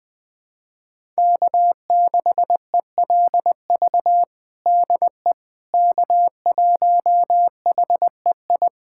Morse code, heard as a keyed tone, about 19 wpm, 700 Hz. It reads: K6ELV DE K1HEI
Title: USA Callsign DE Callsign - 5 Characters - 20wpm